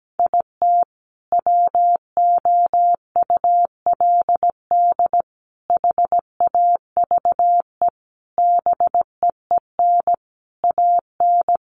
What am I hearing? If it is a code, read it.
Morse code, heard as a keyed tone, about 17 wpm, 700 Hz. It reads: IT WOULD HAVE BEEN AN
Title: Sentences from Top 1100 Words 17wpm